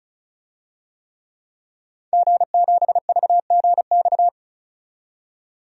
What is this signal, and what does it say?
Morse code, heard as a keyed tone, about 36 wpm, 700 Hz. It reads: G7VGX
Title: DXCC-100 Call Signs - 5 Characters - 35wpm